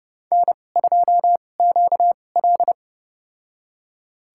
Morse code, heard as a keyed tone, about 30 wpm, spaced as for 22 wpm, 700 Hz. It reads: N2QL
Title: USA Call Signs - 4 Characters - 2x Character Spacing - 30wpm